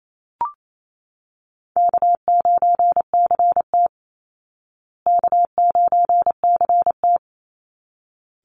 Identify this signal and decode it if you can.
Morse code, heard as a keyed tone, about 27 wpm, 700 Hz. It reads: K9CT K9CT
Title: Easy Call Signs Repeated 2x - 28wpm